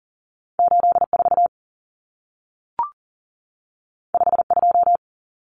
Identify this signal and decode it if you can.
Morse code, heard as a keyed tone, about 40 wpm, 700 Hz. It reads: 84 52
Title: Numbers - 2-Digit 40wpm